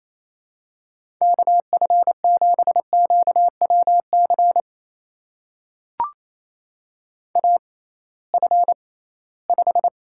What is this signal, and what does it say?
Morse code, heard as a keyed tone, about 28 wpm, spaced as for 18 wpm, 700 Hz. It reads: KF7QWC AF5
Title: USA Call Signs - 6x Character Spacing - 28wpm